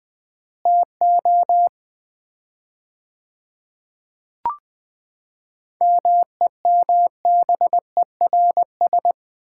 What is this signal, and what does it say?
Morse code, heard as a keyed tone, about 20 wpm, 700 Hz. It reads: TO MEMBERS
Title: 4x Word-Spacing - Sets of 3 Words - 20wpm